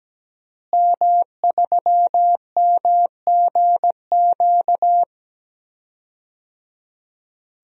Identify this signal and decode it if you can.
Morse code, heard as a keyed tone, about 17 wpm, 700 Hz. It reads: M3MGQ
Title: DXCC-100 Call Signs - 5 Characters - Repeated 3x - 17wpm